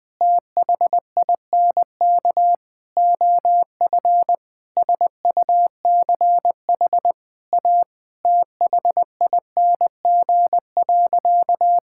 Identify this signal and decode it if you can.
Morse code, heard as a keyed tone, about 21 wpm, 700 Hz. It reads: THINK OF SUCH A THING.
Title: Sentences from Top 1600 Words 20wpm